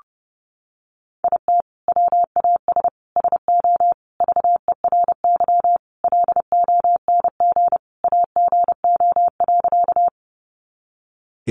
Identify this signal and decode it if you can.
Morse code, heard as a keyed tone, about 30 wpm, 700 Hz. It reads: IT WAS SO VERY LONG AGO.
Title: Sentences from Top 500 Words 30wpm